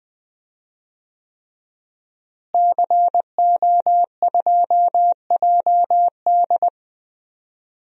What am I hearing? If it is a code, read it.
Morse code, heard as a keyed tone, about 20 wpm, 700 Hz. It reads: CO2JD
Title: DXCC-100 Call Signs - 5 Characters - Repeated 3x - 20wpm